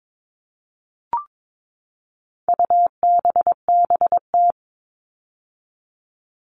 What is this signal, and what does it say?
Morse code, heard as a keyed tone, about 22 wpm, 700 Hz. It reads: UBBT
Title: All Cut Numbers - 4-Digit 22wpm